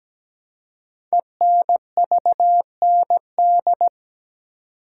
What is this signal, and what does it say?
Morse code, heard as a keyed tone, about 17 wpm, 700 Hz. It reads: ENVND
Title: Code Groups - 5 Letters - 17wpm